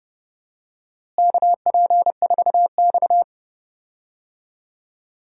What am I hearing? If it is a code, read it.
Morse code, heard as a keyed tone, about 31 wpm, 700 Hz. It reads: KP4X
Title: DXCC-100 Call Signs - 4 Characters - Repeated 3x - 30wpm